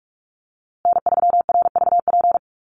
Morse code, heard as a keyed tone, about 44 wpm, 700 Hz. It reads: N3RVP
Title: USA Call Signs Repeated 2x 45wpm